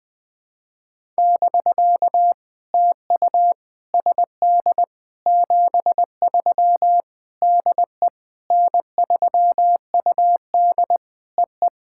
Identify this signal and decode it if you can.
Morse code, heard as a keyed tone, about 20 wpm, 700 Hz. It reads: <BK> TU SD 73 DE N3UD EE
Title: POTA QSO Practice - 20wpm